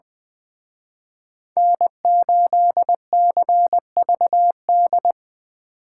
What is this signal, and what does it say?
Morse code, heard as a keyed tone, about 23 wpm, 700 Hz. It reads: N8CVD
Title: Speed Racing - USA 1x3 Call Signs 20wpm